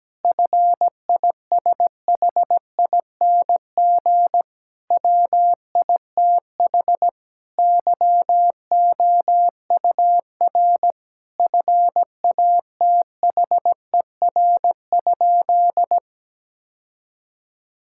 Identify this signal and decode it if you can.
Morse code, heard as a keyed tone, about 17 wpm, 700 Hz. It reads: FISHING WITH YOUR FATHER?
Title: Sentences from Top 1000 Words - Review Part 5 - 17wpm